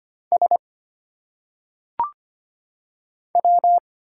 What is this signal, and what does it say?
Morse code, heard as a keyed tone, about 25 wpm, 700 Hz. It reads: S W
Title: Single Letters - Phonetic Pronunciation 25wpm